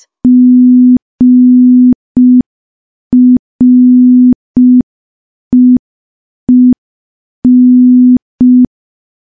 Morse code, heard as a keyed tone, about 5 wpm, 250 Hz. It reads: GREEN
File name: morse / morse-training-05WPM